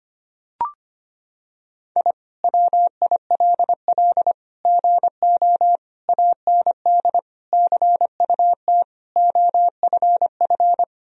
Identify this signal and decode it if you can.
Morse code, heard as a keyed tone, about 25 wpm, 700 Hz. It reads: I WILL GO AND CUT OFF
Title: Sentences from Top 1000 Words - Review Part 1 - 25wpm